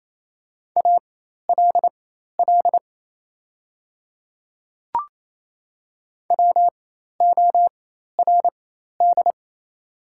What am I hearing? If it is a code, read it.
Morse code, heard as a keyed tone, about 28 wpm, spaced as for 13 wpm, 690 Hz. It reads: ALL WORD
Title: 4x Character Spacing - Top 300 Words - 28wpm